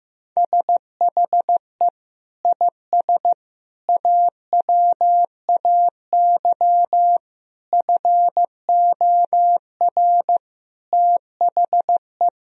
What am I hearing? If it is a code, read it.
Morse code, heard as a keyed tone, about 15 wpm, 700 Hz. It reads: SHE IS AWAY FOR THE
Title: Sentences from Top 1000 Words - Review Part 2 - 15wpm